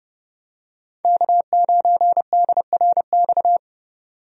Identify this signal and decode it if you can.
Morse code, heard as a keyed tone, about 32 wpm, 700 Hz. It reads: K9DRX
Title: USA Call Signs - 5 Characters - 30wpm